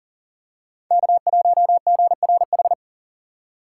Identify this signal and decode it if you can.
Morse code, heard as a keyed tone, about 40 wpm, 700 Hz. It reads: K1GRH
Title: USA Call Signs - 5 Characters - 6x Character Spacing - 40wpm